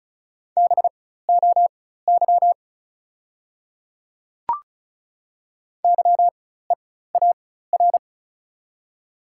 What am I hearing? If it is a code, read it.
Morse code, heard as a keyed tone, about 36 wpm, spaced as for 16 wpm, 690 Hz. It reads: BOY YEAR
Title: 4x Character Spacing - Top 200 Words - 35wpm